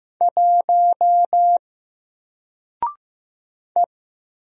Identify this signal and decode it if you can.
Morse code, heard as a keyed tone, about 15 wpm, 700 Hz. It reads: 1 E
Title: Single Letter-Number 15wpm